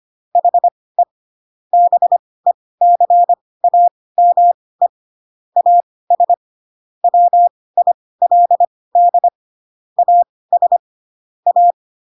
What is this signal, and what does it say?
Morse code, heard as a keyed tone, about 25 wpm, spaced as for 18 wpm, 700 Hz. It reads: HE BECAME AS WILD AS A
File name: Pinocchio / Pinocchio ch2 12 25 ik1hge